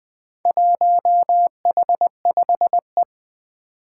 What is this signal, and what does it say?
Morse code, heard as a keyed tone, about 20 wpm, 700 Hz. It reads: 1H5E
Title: Four Challenging Letter-Number Combinations 20wpm